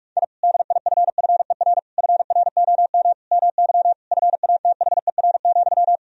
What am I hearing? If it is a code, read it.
Morse code, heard as a keyed tone, about 46 wpm, 700 Hz. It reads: I DIFFER FROM MY FATHER,
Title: Sentences from Top 1000 Words - Review Part 1 - 45wpm